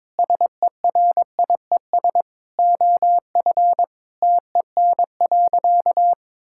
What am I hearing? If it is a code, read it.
Morse code, heard as a keyed tone, about 22 wpm, 700 Hz. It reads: SERIES OF TEN.
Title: Sentences from Top 1000 Words - Review Part 5 - 22wpm